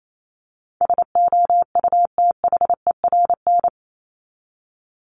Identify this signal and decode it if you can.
Morse code, heard as a keyed tone, about 28 wpm, 700 Hz. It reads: SOUTHERN
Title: Top 1751-2000 Words 28wpm